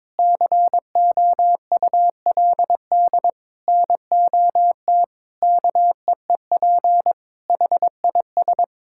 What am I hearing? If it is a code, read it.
Morse code, heard as a keyed tone, about 21 wpm, 700 Hz. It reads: COULD NOT KEEP HIS
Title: Sentences from Top 1000 Words - Review Part 6 - 22wpm